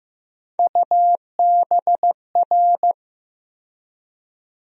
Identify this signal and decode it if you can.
Morse code, heard as a keyed tone, about 15 wpm, 700 Hz. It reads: UBR
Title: USA Call Sign Postfixes - 3 Letters 15wpm